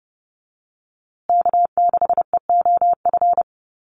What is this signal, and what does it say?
Morse code, heard as a keyed tone, about 29 wpm, 700 Hz. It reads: K6EOF
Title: USA Call Signs Repeated 3x 30wpm